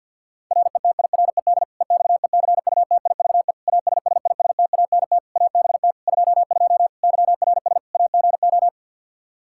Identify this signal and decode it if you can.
Morse code, heard as a keyed tone, about 49 wpm, 700 Hz. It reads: RETIRED EXECUTIVE ASSISTANT ABT 22 YRS AGO
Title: Practice QSO - Exchange 3 - More Info - 50wpm